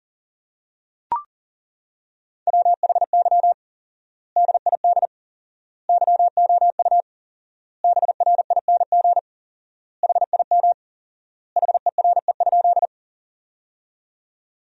Morse code, heard as a keyed tone, about 40 wpm, 700 Hz. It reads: WHY DID YOU BRING HIM HERE?
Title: 4x Word-Spacing - Sentences from Top 500 Words - 40wpm